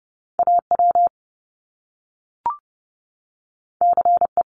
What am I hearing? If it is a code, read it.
Morse code, heard as a keyed tone, about 30 wpm, 700 Hz. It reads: AW CE
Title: Top Two Letter Combinations in Words 30wpm